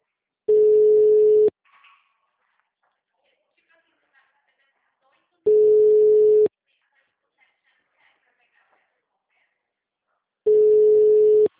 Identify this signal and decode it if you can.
Ringback tone (Europe, 425 Hz), with speech.
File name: out- -2004- - - .44